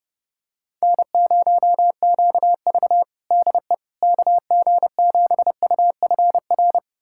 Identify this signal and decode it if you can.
Morse code, heard as a keyed tone, about 30 wpm, 700 Hz. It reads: N0QV DE KG7UFR
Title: USA Callsign DE Callsign 30wpm